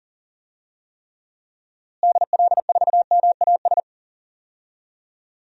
Morse code, heard as a keyed tone, about 37 wpm, 700 Hz. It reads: DL4MAS
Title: DXCC-100 Call Signs - 6 Characters - Repeated 3x - 40wpm